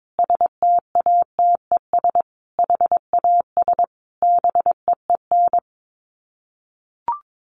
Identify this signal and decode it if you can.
Morse code, heard as a keyed tone, about 22 wpm, 700 Hz. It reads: STATES HAS BEEN
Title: Sets of 5 Words Encore2 22wpm